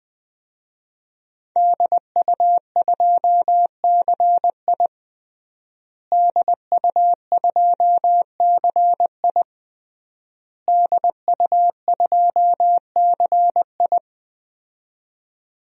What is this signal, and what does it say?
Morse code, heard as a keyed tone, about 20 wpm, 700 Hz. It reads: DU2CI DU2CI DU2CI
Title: DXCC-100 Call Signs - 5 Characters - Repeated 3x - 20wpm